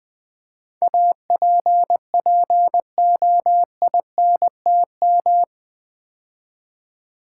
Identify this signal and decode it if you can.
Morse code, heard as a keyed tone, about 20 wpm, 700 Hz. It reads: APPOINTM
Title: Build A Word - 4 Letters At A Time - 11-Letter Words - 20wpm